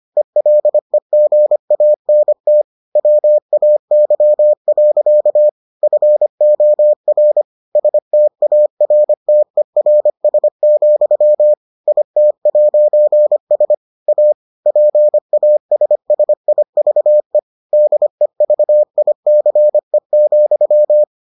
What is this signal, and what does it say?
Morse code, heard as a keyed tone, about 25 wpm, 600 Hz. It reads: ELEGANT WAY. FOR STARTERS, IT'S A PASSIVE DEVICE,